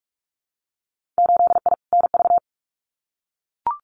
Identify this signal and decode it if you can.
Morse code, heard as a keyed tone, about 46 wpm, 700 Hz. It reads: 8I NV
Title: Winter Field Day Exchanges Encore 45wpm